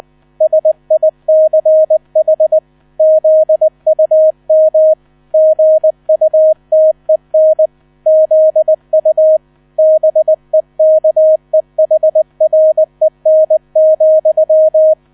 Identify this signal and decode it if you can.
Morse code, heard as a keyed tone, about 20 wpm, 630 Hz. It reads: SICH ZUM GUTEN ZU BEKEHREN,
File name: 000 Vorwort 20wpm